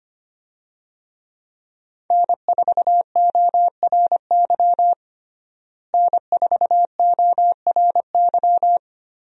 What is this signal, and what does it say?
Morse code, heard as a keyed tone, about 25 wpm, 700 Hz. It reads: N4ORY N4ORY
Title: USA Call Signs - 5 Characters - Repeated 2x - 25wpm